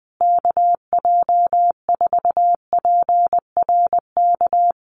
Morse code, heard as a keyed tone, about 19 wpm, 700 Hz. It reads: KJ4PRK
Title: USA Call Signs Repeated 3x 20wpm